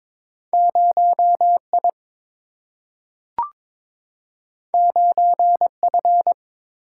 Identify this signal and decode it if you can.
Morse code, heard as a keyed tone, about 21 wpm, 700 Hz. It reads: 0 I 9 F
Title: Single-Letter Single-Number 22wpm